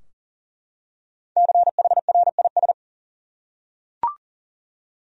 Morse code, heard as a keyed tone, about 40 wpm, 700 Hz. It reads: CHRIS
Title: USA Common Names - 40wpm